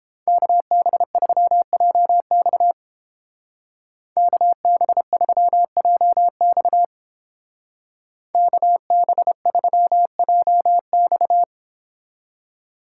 Morse code, heard as a keyed tone, about 29 wpm, 700 Hz. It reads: KB3JX KB3JX KB3JX